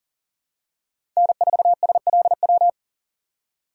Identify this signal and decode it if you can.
Morse code, heard as a keyed tone, about 40 wpm, 700 Hz. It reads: N4SLW
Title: USA Call Signs - 5 Characters - 2x Character Spacing - 40wpm